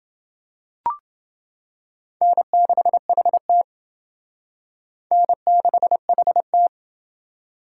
Morse code, heard as a keyed tone, about 28 wpm, 700 Hz. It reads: N6HT N6HT